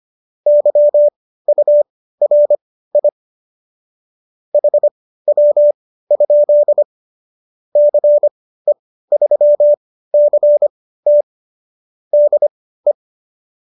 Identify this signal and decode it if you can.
Morse code, heard as a keyed tone, about 25 wpm, spaced as for 15 wpm, 590 Hz. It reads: YURI HW? CE3CT DE